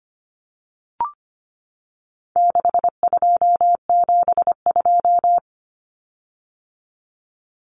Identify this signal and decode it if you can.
Morse code, heard as a keyed tone, about 25 wpm, 700 Hz. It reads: 6272
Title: Common Cut Numbers - 4-Digit 25wpm